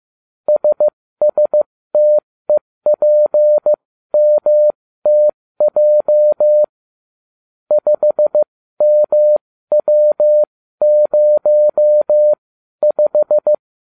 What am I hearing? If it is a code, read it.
Morse code, heard as a keyed tone, about 15 wpm, spaced as for 13 wpm, 600 Hz. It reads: SSTEPMTJ 5MW05